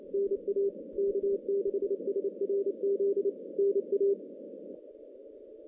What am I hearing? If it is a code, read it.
Morse code, heard as a keyed tone, about 28 wpm, 410 Hz. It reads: NA K6SRZ NA
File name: 2016 JAN CW NAQP / 2016 Jan naqp k5ot - 0-11